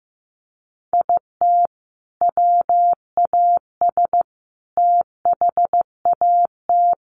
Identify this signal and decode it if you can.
Morse code, heard as a keyed tone, about 15 wpm, 700 Hz. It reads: IT WAS THAT
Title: Sentences from Top 1400 Words 15wpm